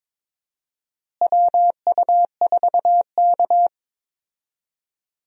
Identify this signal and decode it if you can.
Morse code, heard as a keyed tone, about 22 wpm, 700 Hz. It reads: WU4K
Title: USA Call Signs - 4 Characters - Repeated 3x - 22wpm